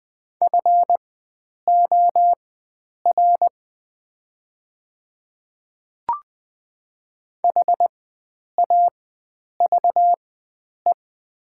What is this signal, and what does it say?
Morse code, heard as a keyed tone, about 20 wpm, spaced as for 9 wpm, 700 Hz. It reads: FOR HAVE
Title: 4x Character Spacing - Top 300 Words - 20wpm